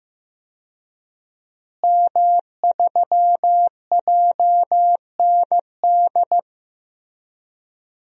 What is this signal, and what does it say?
Morse code, heard as a keyed tone, about 15 wpm, 700 Hz. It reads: M3JND
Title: DXCC-100 Call Signs - 5 Characters - Repeated 3x - 15wpm